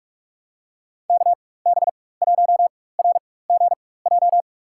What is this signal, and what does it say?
Morse code, heard as a keyed tone, about 42 wpm, spaced as for 20 wpm, 700 Hz. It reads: KB1RGJ
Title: USA Call Signs - 6 Characters - 4x Character Spacing - 45wpm